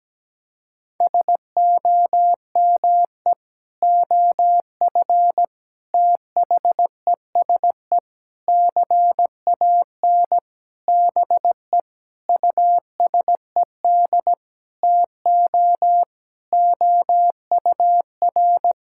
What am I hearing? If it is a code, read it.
Morse code, heard as a keyed tone, about 17 wpm, 700 Hz. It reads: SOME OF THESE CAN BE USED TO OUR
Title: Sentences from Top 1000 Words - Review Part 6 - 17wpm